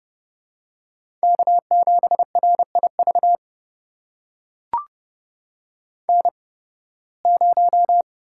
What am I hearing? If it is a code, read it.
Morse code, heard as a keyed tone, about 32 wpm, spaced as for 20 wpm, 700 Hz. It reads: K7RIV N0
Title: USA Call Signs - 5 Characters - 8x Character Spacing - 30wpm